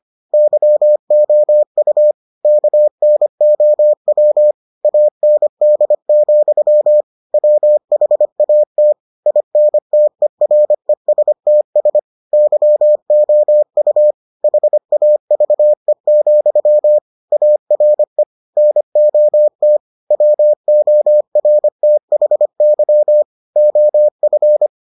Morse code, heard as a keyed tone, about 25 wpm, 600 Hz. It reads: YOU KNOW AND, WHAT INTERESTS YOU HAVE, ARE NOT WORTHY OF